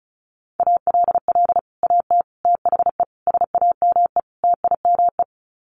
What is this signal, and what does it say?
Morse code, heard as a keyed tone, about 34 wpm, 700 Hz. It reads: ALL AT THE SAME TIME
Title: Sets of 5 Words Encore2 35wpm